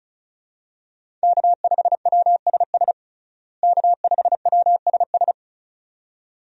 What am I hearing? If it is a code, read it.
Morse code, heard as a keyed tone, about 34 wpm, 700 Hz. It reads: K5WSS K5WSS
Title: USA Call Signs - 5 Characters - Repeated 2x - 35wpm